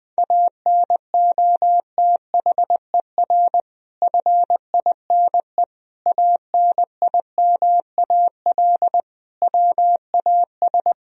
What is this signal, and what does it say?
Morse code, heard as a keyed tone, about 20 wpm, 700 Hz. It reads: ANOTHER FINE ANIMAL WAS
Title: Sentences from Top 1000 Words - Review Part 1 - 20wpm